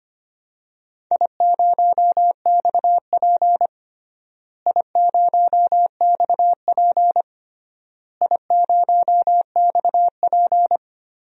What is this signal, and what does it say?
Morse code, heard as a keyed tone, about 25 wpm, 700 Hz. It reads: I0XP I0XP I0XP
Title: DXCC-100 Call Signs - 4 Characters - Repeated 3x - 25wpm